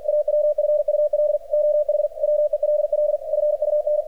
Morse code, heard as a keyed tone, about 44 wpm, 600 Hz, with radio noise. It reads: AWFUL OH WELL FUN
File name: archived / CRSnet